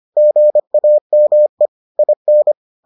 Morse code, heard as a keyed tone, about 25 wpm, 600 Hz. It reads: GAME IN